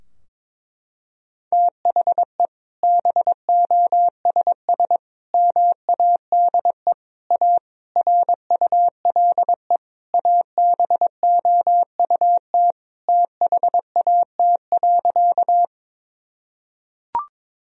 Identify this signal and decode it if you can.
Morse code, heard as a keyed tone, about 22 wpm, 700 Hz. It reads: THE BOSS MADE A RULE ABOUT THAT.
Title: Sentences from Top 1000 Words - Review Part 4 - 22wpm